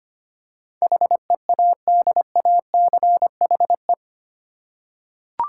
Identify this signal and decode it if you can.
Morse code, heard as a keyed tone, about 25 wpm, 700 Hz. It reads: HEADACHE
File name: 8 Letter Words - First 3 Letters Given - 25wpm